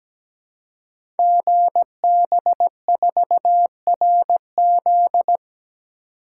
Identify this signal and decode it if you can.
Morse code, heard as a keyed tone, about 17 wpm, 700 Hz. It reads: GB4RZ
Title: Code Groups - 5 Letter:Numbers - 17wpm